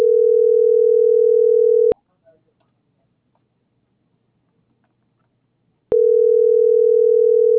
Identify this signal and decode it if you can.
Ringback tone (North America), 440 + 480 Hz.